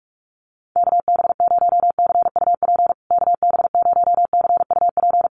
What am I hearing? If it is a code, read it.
Morse code, heard as a keyed tone, about 44 wpm, 700 Hz. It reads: KB0CUP KB0CUP
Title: USA Call Signs Repeated 3x 45wpm